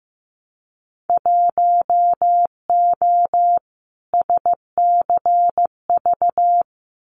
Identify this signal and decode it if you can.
Morse code, heard as a keyed tone, about 15 wpm, 700 Hz. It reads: 1O SCV
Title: Winter Field Day Exchanges 15wpm